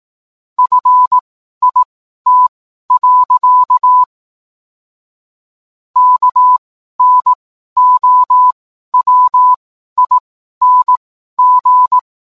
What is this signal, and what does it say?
Morse code, heard as a keyed tone, about 18 wpm, spaced as for 12 wpm, 1000 Hz. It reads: FIT. KNOWING